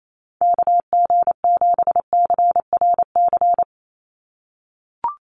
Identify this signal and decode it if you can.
Morse code, heard as a keyed tone, about 28 wpm, 700 Hz. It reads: KG7CRC
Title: USA Call Signs 28wpm